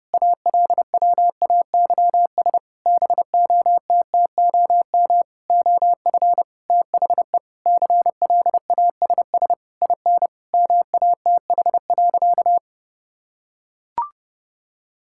Morse code, heard as a keyed tone, about 30 wpm, 700 Hz. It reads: ALWAYS BOTTOM OF THE CLASS IN MATH.
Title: Sentences from Top 1000 Words - Review Part 6 - 30wpm